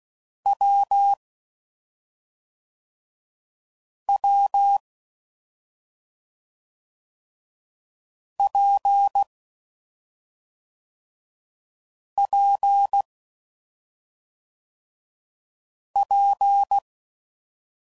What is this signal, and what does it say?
Morse code, heard as a keyed tone, about 16 wpm, 800 Hz. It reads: W W P P P
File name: Combinations / Difficult Combos / AWJ1